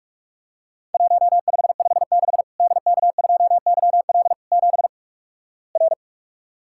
Morse code, heard as a keyed tone, about 44 wpm, 700 Hz. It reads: 1556 DK2YL 7 R
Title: Worked All Europe DX Contest - QTC Exchange - 45wpm